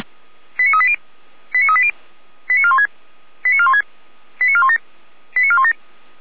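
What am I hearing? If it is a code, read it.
Five-tone selective call (ZVEI), heard as quick runs of five tones, tones of about 70 ms. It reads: 89380 89380 89416 89416 89417 89417
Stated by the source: ZVEI five-tone selective call received by SDR.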